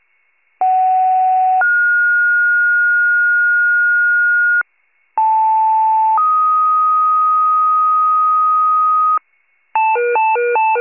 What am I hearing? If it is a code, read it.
Two-tone page, heard as a one-second tone, then a three-second tone, about 750 then 1430 Hz, about 880 then 1250 Hz, with noise. It is heard from 0.6 to 9.2 s.
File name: pager / 76 FIRE / wcalert